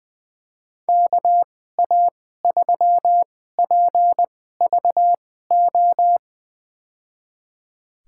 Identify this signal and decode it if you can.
Morse code, heard as a keyed tone, about 20 wpm, spaced as for 14 wpm, 700 Hz. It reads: KA3PVO
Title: USA Call Signs - 6 Characters - 2x Character Spacing - 20wpm